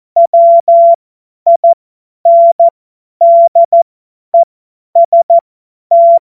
Morse code, heard as a keyed tone, about 14 wpm, spaced as for 10 wpm, 680 Hz. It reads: WINDEST
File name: Morsekurs / Witze / Witz 25. Oktober 2022 50 F70